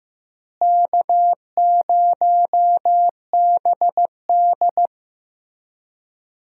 Morse code, heard as a keyed tone, about 16 wpm, 700 Hz. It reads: K0BD
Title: USA Call Signs - 4 Characters - Repeated 2x - 15wpm